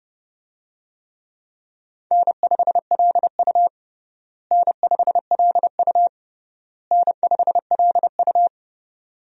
Morse code, heard as a keyed tone, about 30 wpm, 700 Hz. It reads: N5LU N5LU N5LU
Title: USA Call Signs - 4 Characters - Repeated 3x - 30wpm